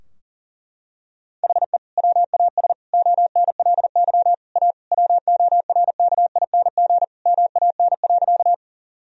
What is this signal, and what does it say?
Morse code, heard as a keyed tone, about 40 wpm, 700 Hz. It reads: HE WAS ONLY A WORKING MAN.
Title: Sentences from Top 1000 Words - Review Part 4 - 40wpm